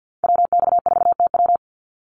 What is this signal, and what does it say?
Morse code, heard as a keyed tone, about 48 wpm, 700 Hz. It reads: WX4TW
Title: NA SOTA Activator Callsigns 50wpm